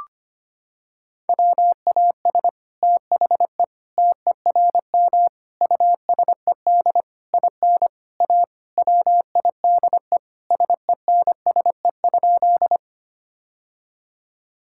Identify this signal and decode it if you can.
Morse code, heard as a keyed tone, about 25 wpm, 700 Hz. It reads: WAS THE TERM USED IN A WIDE SENSE?
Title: Sentences from Top 1000 Words - Review Part 6 - 25wpm